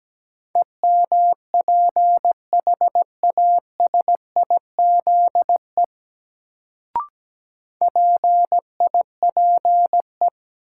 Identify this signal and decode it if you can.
Morse code, heard as a keyed tone, about 17 wpm, 700 Hz. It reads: EMPHASIZE PIPE
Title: Sentences from Top 1500 Words - New Word Drill - 17wpm